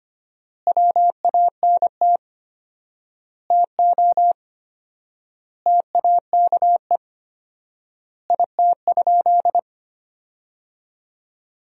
Morse code, heard as a keyed tone, about 25 wpm, 700 Hz. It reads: WANT TO TAKE IT?
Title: 4x Word-Spacing - Sentences from Top 300 Words - 25wpm